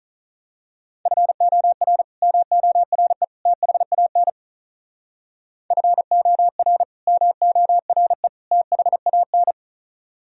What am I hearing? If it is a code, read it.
Morse code, heard as a keyed tone, about 39 wpm, 700 Hz. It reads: FOR MORE THAN FOR MORE THAN
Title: Speed-Racing - Sets of 3 Words 35wpm